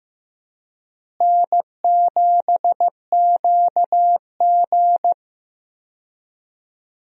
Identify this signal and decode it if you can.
Morse code, heard as a keyed tone, about 15 wpm, 700 Hz. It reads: N7QG
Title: USA Call Signs - 4 Characters - Repeated 2x - 15wpm